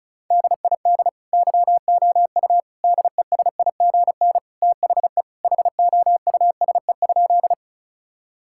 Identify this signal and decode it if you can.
Morse code, heard as a keyed tone, about 36 wpm, 700 Hz. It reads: DID YOU DESIGN THE HOUSE?
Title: Sentences from Top 1000 Words - Review Part 5 - 35wpm